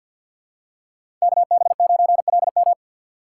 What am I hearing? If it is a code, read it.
Morse code, heard as a keyed tone, about 50 wpm, 700 Hz. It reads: KB9LM
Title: USA Call Signs - 2x Character Spacing - 50wpm